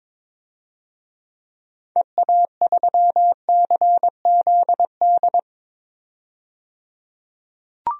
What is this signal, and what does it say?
Morse code, heard as a keyed tone, about 22 wpm, 700 Hz. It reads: EA3CZD
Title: DXCC-100 Call Signs - 6 Characters - 22wpm